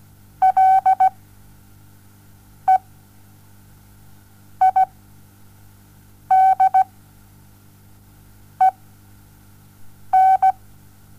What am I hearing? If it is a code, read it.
Morse code, heard as a keyed tone, about 17 wpm, spaced as for 5 wpm, 760 Hz. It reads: LEIDEN
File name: Onlinelehrgang / morsen / DJ4UF Morsen07